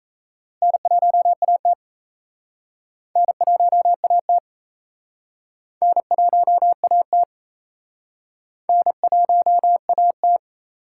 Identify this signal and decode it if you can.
Morse code, heard as a keyed tone, about 37 wpm, 700 Hz. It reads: N1AT N1AT N1AT N1AT